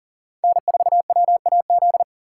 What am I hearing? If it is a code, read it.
Morse code, heard as a keyed tone, about 40 wpm, 700 Hz. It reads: N4WAZ
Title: USA Call Signs - 5 Characters - 4x Character Spacing - 40wpm